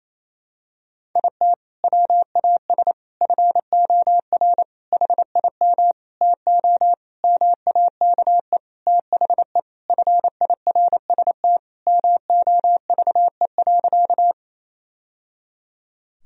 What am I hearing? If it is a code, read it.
Morse code, heard as a keyed tone, about 28 wpm, 700 Hz. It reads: IT WAS FOR HIM TO MAKE THE FIRST MOVE.
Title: Sentences from Top 1000 Words - Review Part 1 - 28wpm